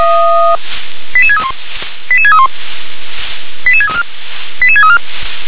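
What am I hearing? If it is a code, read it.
Five-tone selective call (ZVEI), heard as quick runs of five tones, tones of about 70 ms. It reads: 88631 88631 88634 88634
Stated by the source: ZVEI five-tone selective call received with an SDR.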